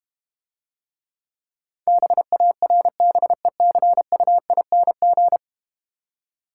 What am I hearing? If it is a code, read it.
Morse code, heard as a keyed tone, about 31 wpm, 690 Hz. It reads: BARBECUING
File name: Easy Words - Speed-Racing with Hints - 25wpm